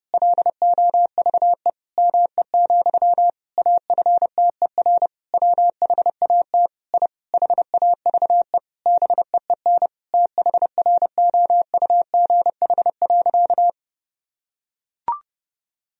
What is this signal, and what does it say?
Morse code, heard as a keyed tone, about 32 wpm, 700 Hz. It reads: LOVE ME, AFTER WHAT I HAVE BEEN THROUGH.
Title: Sentences from Top 1000 Words - Review Part 1 - 30wpm